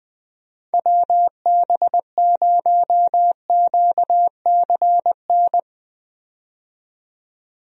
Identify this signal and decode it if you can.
Morse code, heard as a keyed tone, about 20 wpm, 700 Hz. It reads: WB0QCN
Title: USA Call Signs - 6 Characters - 20wpm